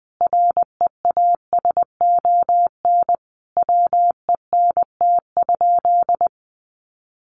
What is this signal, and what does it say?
Morse code, heard as a keyed tone, about 20 wpm, 700 Hz. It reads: REASON WENT?
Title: Sentences from Top 700 Words 20wpm